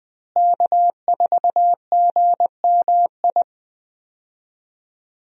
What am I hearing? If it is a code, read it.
Morse code, heard as a keyed tone, about 20 wpm, 700 Hz. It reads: K4GMI
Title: USA Call Signs - 5 Characters - Repeated 2x - 20wpm